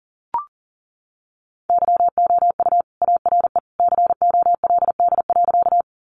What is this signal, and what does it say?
Morse code, heard as a keyed tone, about 40 wpm, 700 Hz. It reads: YOU ARE COLD.
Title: Sentences from Top 400 Words 40wpm